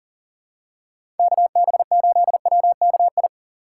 Morse code, heard as a keyed tone, about 40 wpm, 700 Hz. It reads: KB8WKI
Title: USA Call Signs - 6 Characters - 6x Character Spacing - 40wpm